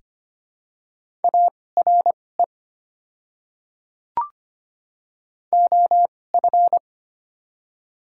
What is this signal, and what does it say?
Morse code, heard as a keyed tone, about 25 wpm, spaced as for 18 wpm, 700 Hz. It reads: ARE OF
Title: 2x Character Spacing - Top 100 Words - 25wpm